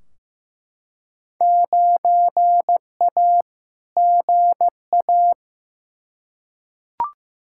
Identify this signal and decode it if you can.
Morse code, heard as a keyed tone, about 15 wpm, 700 Hz. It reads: 9A GA
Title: Field Day Exchanges Encore - 15wpm